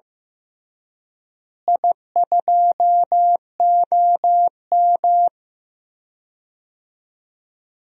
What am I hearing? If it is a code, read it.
Morse code, heard as a keyed tone, about 15 wpm, 700 Hz. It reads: I2OM
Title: DXCC-100 Call Signs - 4 Characters - Repeated 3x - 15wpm